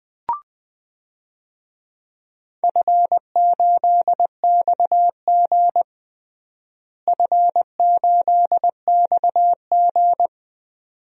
Morse code, heard as a keyed tone, about 21 wpm, 700 Hz. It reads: F8XG F8XG
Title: DXCC-100 Call Signs - 4 Characters - Repeated 2x - 20wpm